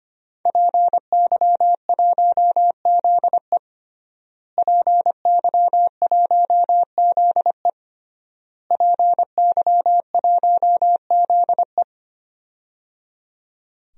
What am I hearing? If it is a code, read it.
Morse code, heard as a keyed tone, about 25 wpm, 700 Hz. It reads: PY1ZE PY1ZE PY1ZE
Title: DXCC-100 Call Signs - 5 Characters - Repeated 3x - 25wpm